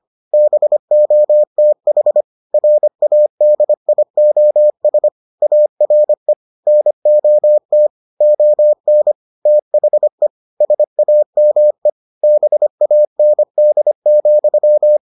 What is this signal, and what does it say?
Morse code, heard as a keyed tone, about 25 wpm, 600 Hz. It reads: BOTH RADIOS ARE NOT ON THE SAME BAND,